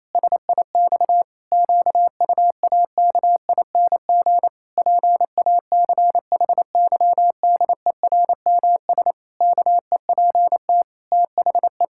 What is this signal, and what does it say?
Morse code, heard as a keyed tone, about 28 wpm, 700 Hz. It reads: SIX QUAKING PACHYDERMS KEPT THE
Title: Pangram Challenge v2 - 28wpm